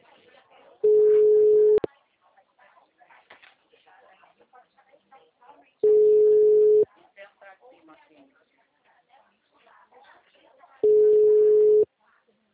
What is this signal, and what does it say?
Ringback tone (Europe, 425 Hz), with speech.
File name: out- -2011- - - .52803